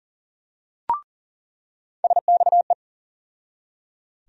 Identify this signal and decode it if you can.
Morse code, heard as a keyed tone, about 40 wpm, 700 Hz. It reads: SXE
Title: USA Call Sign Postfixes - 3 Letters 40wpm